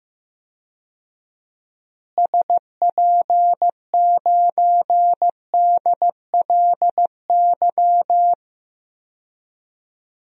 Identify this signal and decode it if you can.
Morse code, heard as a keyed tone, about 15 wpm, 700 Hz. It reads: SP9DLY
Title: DXCC-100 Call Signs - 6 Characters - Repeated 3x - 15wpm